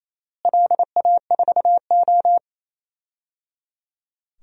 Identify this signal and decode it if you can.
Morse code, heard as a keyed tone, about 28 wpm, 700 Hz. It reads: LA4O
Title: DXCC-100 Call Signs - 4 Characters - Repeated 3x - 28wpm